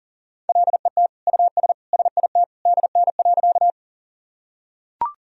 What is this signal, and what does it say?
Morse code, heard as a keyed tone, about 37 wpm, 700 Hz. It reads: LET US SIT DN.
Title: Sentences from Top 500 Words and Abbreviations 40wpm